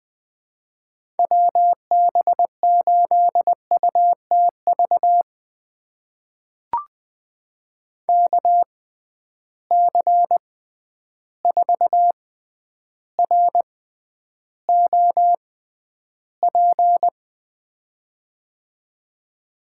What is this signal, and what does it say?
Morse code, heard as a keyed tone, about 20 wpm, spaced as for 10 wpm, 700 Hz. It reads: WB8UTV KC4ROP
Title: USA Call Signs - 6 Characters - 6x Character Spacing - 20wpm